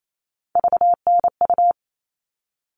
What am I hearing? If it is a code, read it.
Morse code, heard as a keyed tone, about 28 wpm, 700 Hz. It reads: VNU
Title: All Cut Numbers - 3-Digit 28wpm